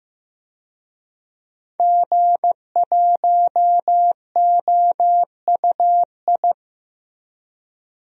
Morse code, heard as a keyed tone, about 15 wpm, 700 Hz. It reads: G1OUI
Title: DXCC-100 Call Signs - 5 Characters - Repeated 2x - 15wpm